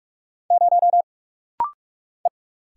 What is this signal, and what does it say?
Morse code, heard as a keyed tone, about 45 wpm, 700 Hz. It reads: 0 E
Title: Single Letter-Number - Rapid-Fire 45wpm